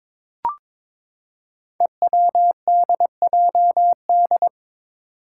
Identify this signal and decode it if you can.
Morse code, heard as a keyed tone, about 23 wpm, 700 Hz. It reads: EWDJD
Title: Five Challenging Letter-Number Combinations 22wpm